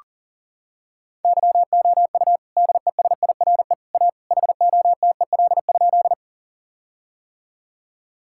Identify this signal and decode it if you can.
Morse code, heard as a keyed tone, about 37 wpm, 700 Hz. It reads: YOU DESIRE A HOTEL?
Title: Sentences from Top 1600 Words 40wpm